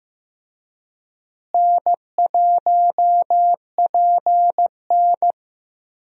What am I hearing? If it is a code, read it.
Morse code, heard as a keyed tone, about 15 wpm, 700 Hz. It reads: N1PN
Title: USA Call Signs - 4 Characters - Repeated 2x - 15wpm